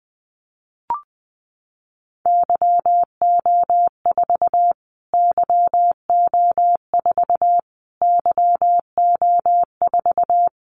Morse code, heard as a keyed tone, about 20 wpm, 700 Hz. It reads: YO4 YO4 YO4
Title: Callsign Prefixes for Top 25 DXCC - Repeated 3x - 20wpm